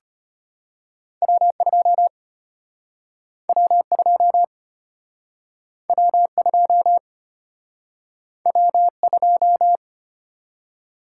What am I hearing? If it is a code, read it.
Morse code, heard as a keyed tone, about 34 wpm, 700 Hz. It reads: W2 W2 W2 W2